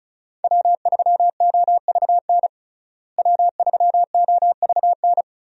Morse code, heard as a keyed tone, about 36 wpm, 700 Hz. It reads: W3OVN W3OVN
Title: USA Call Signs - 5 Characters - Repeated 3x - 35wpm